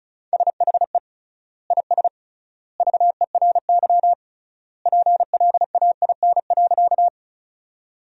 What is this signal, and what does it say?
Morse code, heard as a keyed tone, about 36 wpm, 700 Hz. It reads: SHE IS VERY PLAIN.
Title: 3x Word-Spacing - Sentences from Top 400 Words - 35wpm